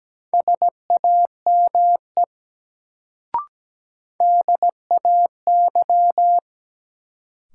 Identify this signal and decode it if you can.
Morse code, heard as a keyed tone, about 17 wpm, 700 Hz. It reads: SAME DAY
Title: Sentences from Top 200 Words - New Word Drill - 17wpm